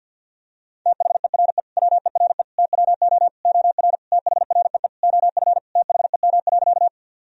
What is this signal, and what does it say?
Morse code, heard as a keyed tone, about 50 wpm, 700 Hz. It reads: THERE WERE TWO OR THREE OF THEM.
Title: Sentences from Top 1000 Words - Review Part 1 - 50wpm